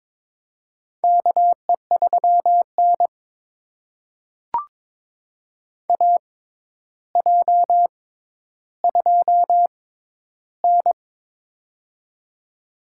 Morse code, heard as a keyed tone, about 22 wpm, spaced as for 11 wpm, 700 Hz. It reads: KE3N AJ2N
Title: USA Call Signs - 4 Characters - 6x Character Spacing - 22wpm